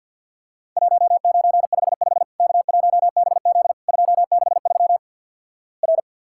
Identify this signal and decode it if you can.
Morse code, heard as a keyed tone, about 48 wpm, 700 Hz. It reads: 1955 K1BZ 263 R
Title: Worked All Europe DX Contest - QTC Exchange - 50wpm